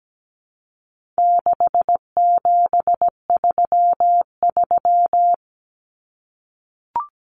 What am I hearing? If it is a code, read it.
Morse code, heard as a keyed tone, about 17 wpm, 700 Hz. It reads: 6733
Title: Numbers - 4-Digit 17wpm